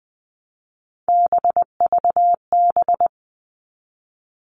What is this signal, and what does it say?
Morse code, heard as a keyed tone, about 19 wpm, 700 Hz. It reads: BVB
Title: All Cut Numbers - 3-Digit 20wpm